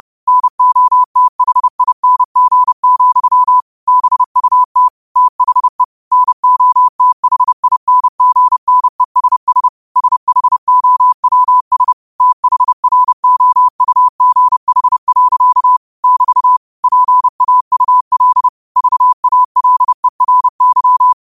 Morse code, heard as a keyed tone, about 30 wpm, 1000 Hz. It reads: NOTHING, BUT THE NOTHINGNESS SHOWS THROUGH. <BT> PAUL VALERY
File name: qotdfiles / 30wpm